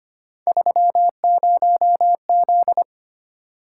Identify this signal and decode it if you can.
Morse code, heard as a keyed tone, about 25 wpm, 700 Hz. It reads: 30Z
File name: Three Letter-Number Combinations - 25wpm